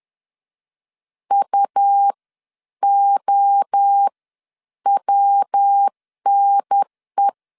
Morse code, heard as a keyed tone, about 10 wpm, spaced as for 8 wpm, 800 Hz. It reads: UOWNE
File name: 10wpm